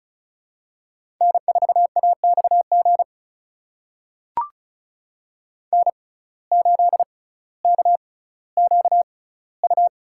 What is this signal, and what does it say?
Morse code, heard as a keyed tone, about 34 wpm, spaced as for 18 wpm, 700 Hz. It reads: N4AXG N8KQU
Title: USA Call Signs - 5 Characters - 6x Character Spacing - 35wpm